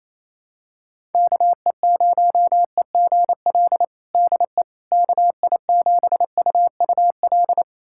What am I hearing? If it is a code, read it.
Morse code, heard as a keyed tone, about 28 wpm, 700 Hz. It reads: KE0EGL DE KI7UUL
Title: USA Callsign DE Callsign 28wpm